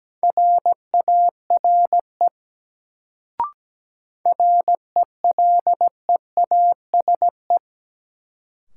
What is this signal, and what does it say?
Morse code, heard as a keyed tone, about 17 wpm, 700 Hz. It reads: RARE RELEASE
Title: Sentences from Top 1100 Words - New Word Drill Alt - 17wpm